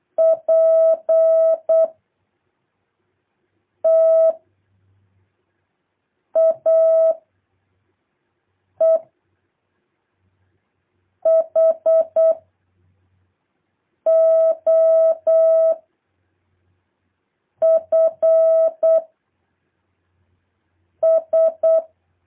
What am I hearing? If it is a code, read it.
Morse code, heard as a keyed tone, about 8 wpm, 650 Hz. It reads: P T A E H O F S
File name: jh8cbh / 2023cw